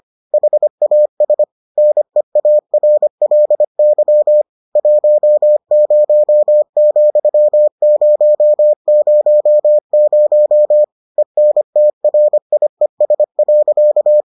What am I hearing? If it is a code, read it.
Morse code, heard as a keyed tone, about 25 wpm, 600 Hz. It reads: HAS NEARLY 10,000 ENTRIES.